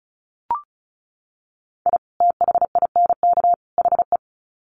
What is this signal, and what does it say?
Morse code, heard as a keyed tone, about 34 wpm, 700 Hz. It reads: I THINK HE
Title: Sets of 3 Words 35wpm